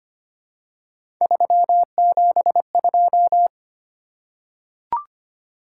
Morse code, heard as a keyed tone, about 25 wpm, 700 Hz. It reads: 372
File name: Three Letter-Number Combinations - 25wpm